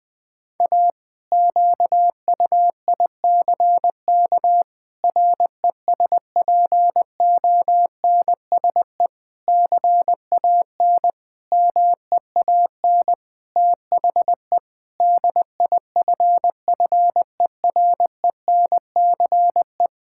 Morse code, heard as a keyed tone, about 20 wpm, 700 Hz. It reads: A QUICK RESPONSE CAN MEAN THE DIFFERENCE
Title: Sentences with 8-Word Phrases - Spoken Beginning - 20wpm